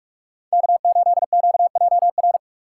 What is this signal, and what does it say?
Morse code, heard as a keyed tone, about 46 wpm, 700 Hz. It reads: K8QJR
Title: USA Call Signs - 5 Characters - 3x Character Spacing - 45wpm